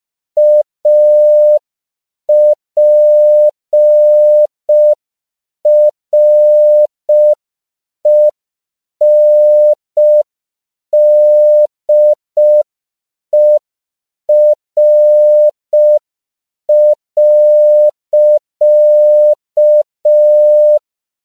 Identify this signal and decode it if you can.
Morse code, heard as a keyed tone, about 5 wpm, 600 Hz. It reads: APRENDER.
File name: CW / 5PPM